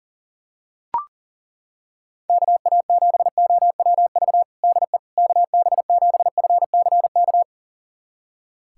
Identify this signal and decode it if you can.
Morse code, heard as a keyed tone, about 40 wpm, 700 Hz. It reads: KA7OWV DE KB7FCK
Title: USA Callsign DE Callsign - 6 Characters - 40wpm